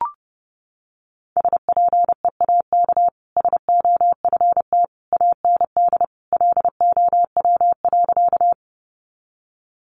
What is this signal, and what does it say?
Morse code, heard as a keyed tone, about 30 wpm, 700 Hz. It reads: SPEAK SOFT AND LOW.
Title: Sentences from Top 1500 Words 30wpm